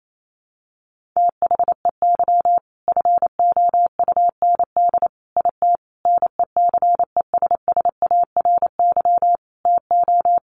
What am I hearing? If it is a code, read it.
Morse code, heard as a keyed tone, about 28 wpm, 700 Hz. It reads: THEY FOUND IT NECESSARY TO
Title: Sentences from Top 1400 Words 28wpm